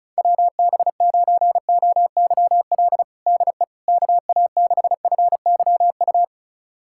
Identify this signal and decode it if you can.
Morse code, heard as a keyed tone, about 36 wpm, 700 Hz. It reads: WB9OYL DE KA6FYU
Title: USA Callsign DE Callsign - 6 Characters - 35wpm